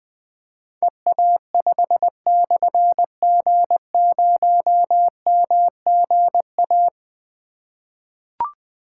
Morse code, heard as a keyed tone, about 20 wpm, 700 Hz. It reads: EA5/G0MGA
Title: Difficult Call Signs 20wpm